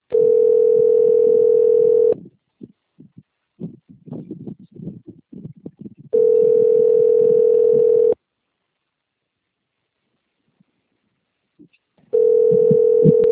Ringback tone (North America), 440 + 480 Hz.